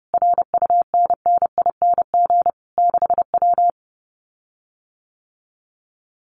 Morse code, heard as a keyed tone, about 30 wpm, 700 Hz. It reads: RUNNING 6W
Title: Common QSO Phrases Encore 30wpm